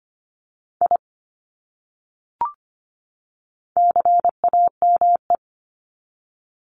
Morse code, heard as a keyed tone, about 25 wpm, 700 Hz. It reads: I CAME
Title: Top 300 Words 25wpm